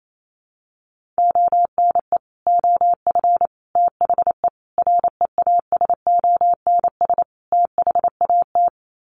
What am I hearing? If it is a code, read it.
Morse code, heard as a keyed tone, about 28 wpm, 700 Hz. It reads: ONE OF THE REASONS THAT
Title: Sets of 5 Words Encore 28wpm